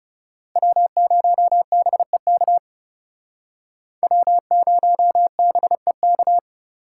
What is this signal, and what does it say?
Morse code, heard as a keyed tone, about 35 wpm, 700 Hz. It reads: W0BEK W0BEK
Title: Speed Racing - USA 1x3 Call Signs 30wpm